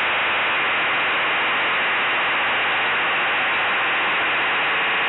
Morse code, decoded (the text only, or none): none